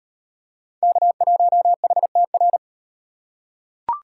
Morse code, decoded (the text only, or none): K1HTR